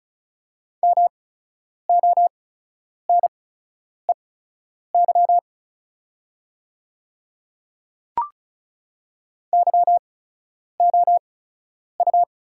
Morse code, heard as a keyed tone, about 36 wpm, spaced as for 10 wpm, 700 Hz. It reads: MONEY YOU
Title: 8x Character Spacing - Top 500 Words - 35wpm